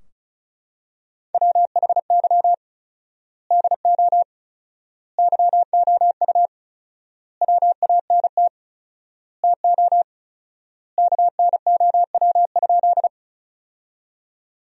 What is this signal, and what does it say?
Morse code, heard as a keyed tone, about 34 wpm, 700 Hz. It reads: WHY DO YOU WANT TO KNOW?
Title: 4x Word-Spacing - Sentences from Top 200 Words - 35wpm